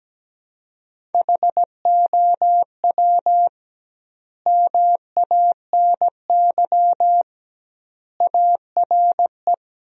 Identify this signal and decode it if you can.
Morse code, heard as a keyed tone, about 17 wpm, 700 Hz. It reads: HOW MANY ARE
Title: 2x Word-Spacing - Sentences from Top 300 Words - 17wpm